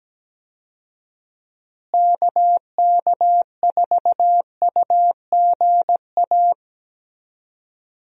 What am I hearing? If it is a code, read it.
Morse code, heard as a keyed tone, about 17 wpm, 700 Hz. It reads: KK4UGA
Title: USA Call Signs - 6 Characters - Repeated 3x - 17wpm